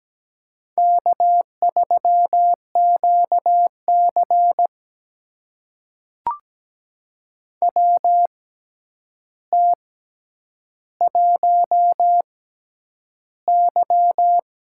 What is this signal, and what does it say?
Morse code, heard as a keyed tone, about 17 wpm, spaced as for 9 wpm, 700 Hz. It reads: K3QC WT1Y
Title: USA Call Signs - 4 Characters - 6x Character Spacing - 17wpm